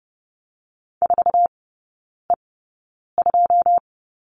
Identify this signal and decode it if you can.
Morse code, heard as a keyed tone, about 30 wpm, spaced as for 9 wpm, 700 Hz. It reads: 4E2
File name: 081 Lesson 14e - Number 2 - Three Characters 30wpm